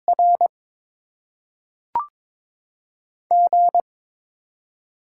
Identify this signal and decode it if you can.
Morse code, heard as a keyed tone, about 22 wpm, 700 Hz. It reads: R G
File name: Top 20 QSO Elements - 8x Character Spacing - 22wpm